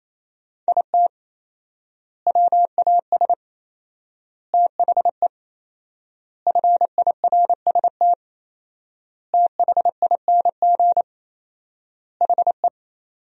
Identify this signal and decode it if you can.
Morse code, heard as a keyed tone, about 27 wpm, 700 Hz. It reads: IT WAS THE FIRST THING HE
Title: 4x Word-Spacing - Sentences from Top 300 Words - 28wpm